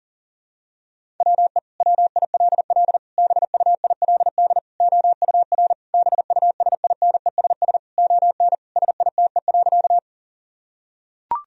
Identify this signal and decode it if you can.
Morse code, heard as a keyed tone, about 37 wpm, 700 Hz. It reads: WE WILL BUILD OUR BUSINESS ON SITE.
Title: Sentences from Top 1000 Words - Review Part 4 - 40wpm